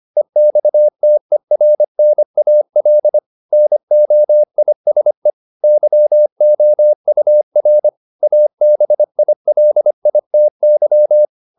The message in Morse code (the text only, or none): EXTERNAL NOISE YOUR ABILITY